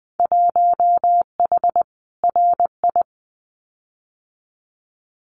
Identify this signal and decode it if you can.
Morse code, heard as a keyed tone, about 20 wpm, 700 Hz. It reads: 1H RI
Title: Winter Field Day Exchanges 20wpm